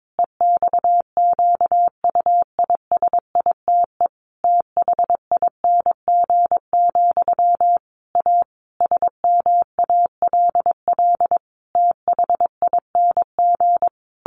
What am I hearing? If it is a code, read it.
Morse code, heard as a keyed tone, about 22 wpm, 700 Hz. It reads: EXQUISITE THING, A SMALL THING
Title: A Sound Of Thunder 22wpm